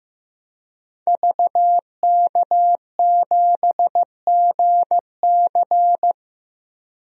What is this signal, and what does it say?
Morse code, heard as a keyed tone, about 15 wpm, 700 Hz. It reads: VK7GC
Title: DXCC-100 Call Signs - 5 Characters - Repeated 3x - 15wpm